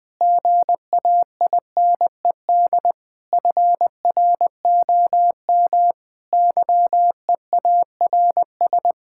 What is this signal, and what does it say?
Morse code, heard as a keyed tone, about 20 wpm, 700 Hz. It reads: GAINED FROM YEARS
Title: Difficult Sentences with Hints - 20wpm